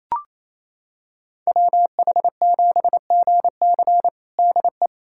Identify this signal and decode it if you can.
Morse code, heard as a keyed tone, about 28 wpm, 700 Hz. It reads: WH7GC DE
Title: USA Callsign DE Callsign - 5 Characters - 28wpm